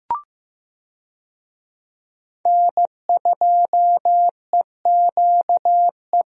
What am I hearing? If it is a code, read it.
Morse code, heard as a keyed tone, about 15 wpm, 700 Hz. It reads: N2EQE
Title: USA Call Signs - 5 Characters - Repeated 2x - 15wpm